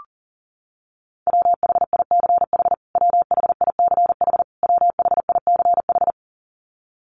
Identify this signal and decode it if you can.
Morse code, heard as a keyed tone, about 40 wpm, 700 Hz. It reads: WHICH WHICH WHICH
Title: Top 500 Words Repeated 40wpm